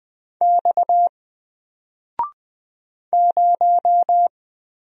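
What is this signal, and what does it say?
Morse code, heard as a keyed tone, about 20 wpm, 700 Hz. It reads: X 0
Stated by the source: Single letter and number Morse drill at 20 wpm, from N4LCD.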